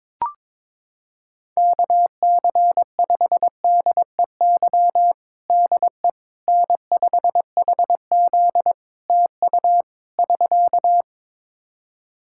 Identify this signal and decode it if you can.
Morse code, heard as a keyed tone, about 22 wpm, 700 Hz. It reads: KC5DEY DE N5HZ TU <SK>
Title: Practice QSO - Exchange 4 - Ending - 22wpm